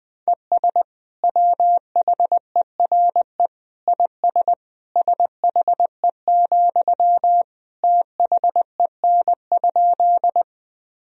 Morse code, heard as a keyed tone, about 20 wpm, 700 Hz. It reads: ES WHERE IS SHE, THEN?
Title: Sentences from Top 200 Words and Abbreviations 20wpm